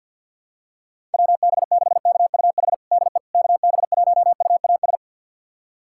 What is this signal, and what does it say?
Morse code, heard as a keyed tone, about 52 wpm, 700 Hz. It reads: WB6KUH DE KB1UAS
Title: USA Callsign DE Callsign - 6 Characters - 50wpm